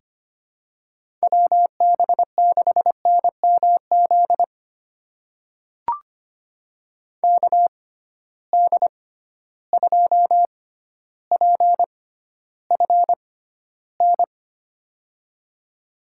Morse code, heard as a keyed tone, about 25 wpm, spaced as for 13 wpm, 700 Hz. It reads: WB6NMZ KD2PFN
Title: USA Call Signs - 6 Characters - 6x Character Spacing - 25wpm